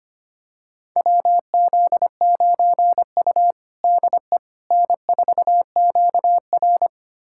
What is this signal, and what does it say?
Morse code, heard as a keyed tone, about 25 wpm, 700 Hz. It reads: WZ9U DE N4QR
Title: USA Callsign DE Callsign - 4 Characters - 25wpm